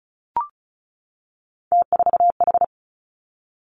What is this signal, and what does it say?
Morse code, heard as a keyed tone, about 36 wpm, 700 Hz. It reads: T4H
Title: Three Challenging Letter-Number Combinations 35wpm